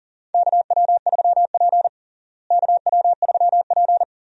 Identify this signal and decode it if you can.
Morse code, heard as a keyed tone, about 37 wpm, 700 Hz. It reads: KW3P KW3P
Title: USA Call Signs - 4 Characters - Repeated 2x - 40wpm